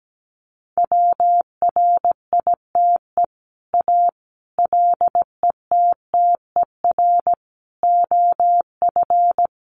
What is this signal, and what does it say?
Morse code, heard as a keyed tone, about 17 wpm, 700 Hz. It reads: WRITE A LETTER OF
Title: Sentences from Top 1400 Words 17wpm